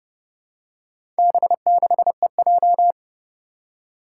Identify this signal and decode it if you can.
Morse code, heard as a keyed tone, about 30 wpm, 700 Hz. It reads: B6EJ
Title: Four Challenging Letter-Number Combinations 30wpm